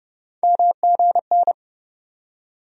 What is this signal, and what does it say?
Morse code, heard as a keyed tone, about 30 wpm, 700 Hz. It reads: MGN
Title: USA Call Sign Postfixes - 3 Letters 30wpm